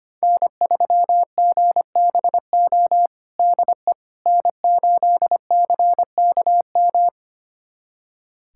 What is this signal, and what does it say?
Morse code, heard as a keyed tone, about 25 wpm, 700 Hz. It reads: N3GBO DE N8CKM
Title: USA Callsign DE Callsign - 5 Characters - 25wpm